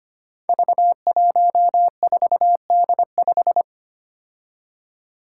Morse code, heard as a keyed tone, about 25 wpm, 700 Hz. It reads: V14D5
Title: Five Challenging Letter-Number Combinations 25wpm